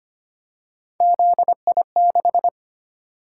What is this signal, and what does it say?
Morse code, heard as a keyed tone, about 29 wpm, 700 Hz. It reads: ZI6
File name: Speed-Racing - Three Letter-Number Combinations - 25wpm